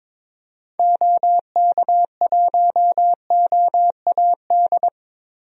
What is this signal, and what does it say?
Morse code, heard as a keyed tone, about 22 wpm, 700 Hz. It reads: OK1OAD
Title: DXCC-100 Call Signs - 6 Characters - 22wpm